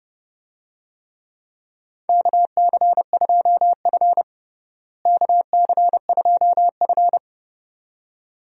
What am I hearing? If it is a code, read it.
Morse code, heard as a keyed tone, about 30 wpm, 700 Hz. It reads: KC2F KC2F
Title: USA Call Signs - 4 Characters - Repeated 2x - 30wpm